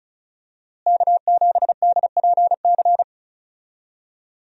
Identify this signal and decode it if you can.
Morse code, heard as a keyed tone, about 36 wpm, 700 Hz. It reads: K7DPC
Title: USA Call Signs - 5 Characters - Repeated 2x - 35wpm